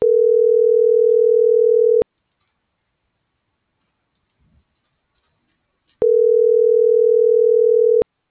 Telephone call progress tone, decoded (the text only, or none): ringback tone